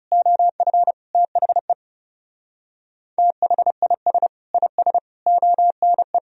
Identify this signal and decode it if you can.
Morse code, heard as a keyed tone, about 34 wpm, 700 Hz. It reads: OF THE THIS IS ONE
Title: Speed-Racing - Sets of 5 Words 30wpm